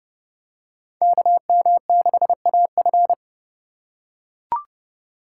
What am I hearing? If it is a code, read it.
Morse code, heard as a keyed tone, about 30 wpm, 700 Hz. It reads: KM6AF
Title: USA Call Signs - 5 Characters - 2x Character Spacing - 30wpm